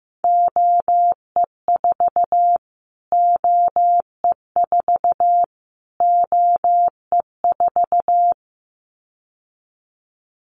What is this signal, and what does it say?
Morse code, heard as a keyed tone, about 15 wpm, 700 Hz. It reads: OE4 OE4 OE4
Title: Callsign Prefixes for Top 25 DXCC - Repeated 3x - 15wpm